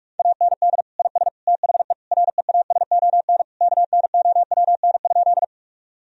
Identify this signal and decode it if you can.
Morse code, heard as a keyed tone, about 44 wpm, 700 Hz. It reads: AND IS THE REASON KNOWN?
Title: Sentences from Top 1000 Words - Review Part 3 - 45wpm